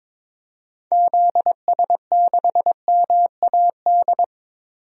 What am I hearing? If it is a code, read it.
Morse code, heard as a keyed tone, about 22 wpm, 700 Hz. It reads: ZS6MAD
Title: DXCC-100 Call Signs - 6 Characters - Repeated 2x - 22wpm